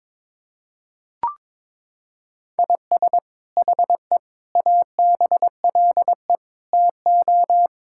Morse code, heard as a keyed tone, about 22 wpm, 700 Hz. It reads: IS HE ABLE TO
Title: Sentences from Top 1000 Words - Review Part 5 - 22wpm